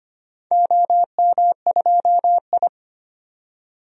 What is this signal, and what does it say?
Morse code, heard as a keyed tone, about 25 wpm, 700 Hz. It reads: OM2I
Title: DXCC-100 Call Signs - 4 Characters - 25wpm